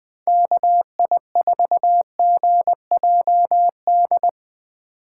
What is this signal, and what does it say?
Morse code, heard as a keyed tone, about 21 wpm, 700 Hz. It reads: KI4GJD
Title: USA Call Signs - 6 Characters - 3x Character Spacing - 20wpm